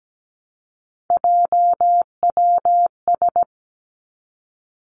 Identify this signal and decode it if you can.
Morse code, heard as a keyed tone, about 17 wpm, 700 Hz. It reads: JWS
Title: Three Challenging Letter-Number Combinations 17wpm